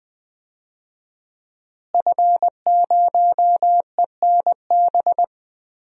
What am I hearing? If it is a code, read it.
Morse code, heard as a keyed tone, about 20 wpm, 700 Hz. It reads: F0ENB
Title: DXCC-100 Call Signs - 5 Characters - Repeated 3x - 20wpm